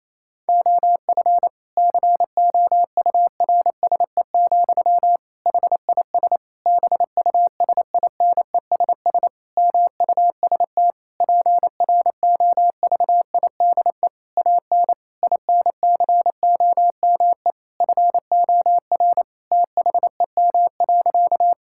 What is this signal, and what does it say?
Morse code, heard as a keyed tone, about 28 wpm, 700 Hz. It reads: OF COURSE, HIS BUSINESS MUST PROVIDE AN INCOME FOR THEM.
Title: Sentences from Top 1000 Words - Review Part 5 - 28wpm